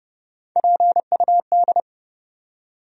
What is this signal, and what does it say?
Morse code, heard as a keyed tone, about 30 wpm, 700 Hz. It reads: PUD
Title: USA Call Sign Postfixes - 3 Letters 30wpm